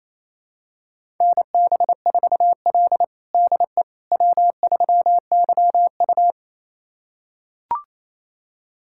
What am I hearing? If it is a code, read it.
Morse code, heard as a keyed tone, about 28 wpm, 700 Hz. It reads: NB4L DE W3YU
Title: USA Callsign DE Callsign - 4 Characters - 28wpm